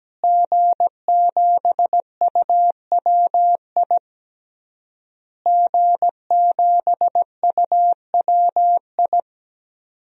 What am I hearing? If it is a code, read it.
Morse code, heard as a keyed tone, about 17 wpm, 700 Hz. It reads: G7UWI G7UWI
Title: DXCC-100 Call Signs - 5 Characters - Repeated 3x - 17wpm